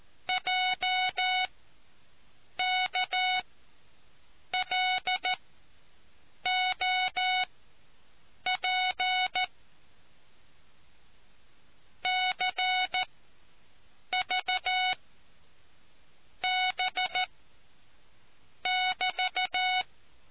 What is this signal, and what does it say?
Morse code, heard as a keyed tone, about 14 wpm, 740 Hz. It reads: J K L O P C V B <BT>